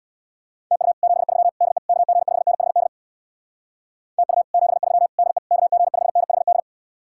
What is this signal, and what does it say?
Morse code, heard as a keyed tone, about 62 wpm, 700 Hz. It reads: TU 73 DE KG5NHD TU 73 DE KG5NHD
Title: Speed-Racing Common QSO Phrases - 50wpm